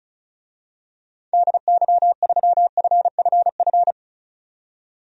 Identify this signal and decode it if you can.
Morse code, heard as a keyed tone, about 36 wpm, 700 Hz. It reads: DY3FFF
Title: DXCC-100 Call Signs - 6 Characters - 35wpm